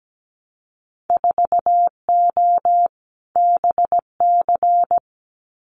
Morse code, heard as a keyed tone, about 17 wpm, 700 Hz. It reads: 4O BC
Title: Winter Field Day Exchanges Encore 17wpm